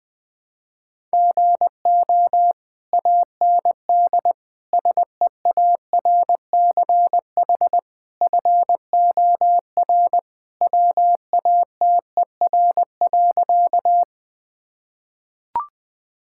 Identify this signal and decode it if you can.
Morse code, heard as a keyed tone, about 21 wpm, 700 Hz. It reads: GO AND SEARCH FOR WATER.
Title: Sentences from Top 1000 Words - Review Part 5 - 20wpm